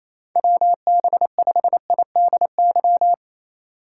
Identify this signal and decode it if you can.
Morse code, heard as a keyed tone, about 27 wpm, 700 Hz. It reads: WB5IDY
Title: K1USN SST QSO Practice - 28wpm